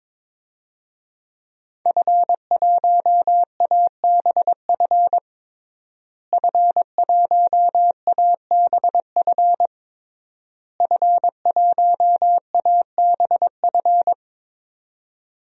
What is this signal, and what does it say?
Morse code, heard as a keyed tone, about 22 wpm, 700 Hz. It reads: F1ABF F1ABF F1ABF
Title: DXCC-100 Call Signs - 5 Characters - Repeated 3x - 22wpm